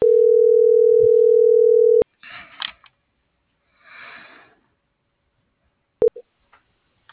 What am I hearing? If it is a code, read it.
Ringback tone (North America), 440 + 480 Hz.
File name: CENTRAL MABE / out edwin.sanchez